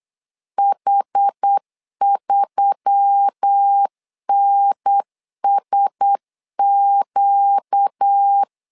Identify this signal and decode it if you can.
Morse code, heard as a keyed tone, about 8 wpm, 800 Hz. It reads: H3NSQ
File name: LEZIONI / 4 Q1S3 8wpm 3037